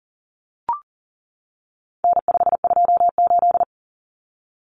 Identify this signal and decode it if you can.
Morse code, heard as a keyed tone, about 40 wpm, 700 Hz. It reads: N528
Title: Common Cut Numbers - 4-Digit 40wpm